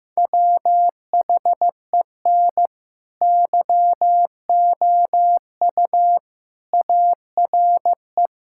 Morse code, heard as a keyed tone, about 15 wpm, 700 Hz. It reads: WHEN YOU ARE
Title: Sentences from Top 1600 Words 15wpm